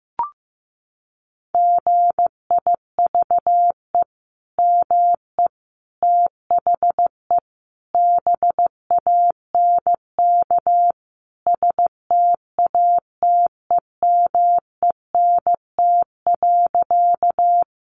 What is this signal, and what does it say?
Morse code, heard as a keyed tone, about 15 wpm, 700 Hz. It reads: GIVE ME THE BANK STATEMENT.
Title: Sentences from Top 1200 Words 15wpm